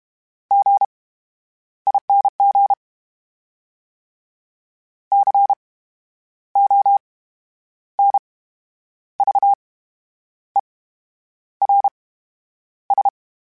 Morse code, heard as a keyed tone, about 32 wpm, 800 Hz. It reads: G I N G C O N V E R S
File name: Special INGs / Special INGs-mp3s / VerySlowToFaster Ing @30